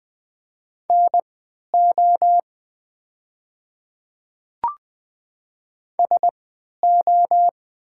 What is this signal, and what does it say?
Morse code, heard as a keyed tone, about 21 wpm, spaced as for 12 wpm, 700 Hz. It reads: NO SO
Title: 3x Character Spacing - Top 100 Words - 20wpm